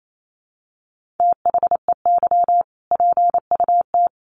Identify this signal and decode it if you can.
Morse code, heard as a keyed tone, about 28 wpm, 700 Hz. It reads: THEY PUT
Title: Sentences from Top 1400 Words 28wpm